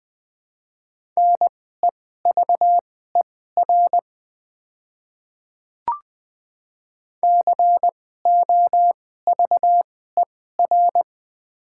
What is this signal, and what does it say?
Morse code, heard as a keyed tone, about 20 wpm, spaced as for 14 wpm, 700 Hz. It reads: NEVER COVER
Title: 2x Character Spacing - Top 500 Words - 20wpm